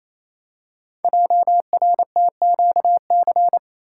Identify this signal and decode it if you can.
Morse code, heard as a keyed tone, about 28 wpm, 700 Hz. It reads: JRTQC
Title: Code Groups - 5 Letters - 28wpm